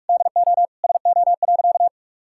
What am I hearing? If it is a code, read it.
Morse code, heard as a keyed tone, about 46 wpm, 700 Hz. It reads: DO SO.
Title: Sentences from Top 1000 Words - Review Part 6 - 45wpm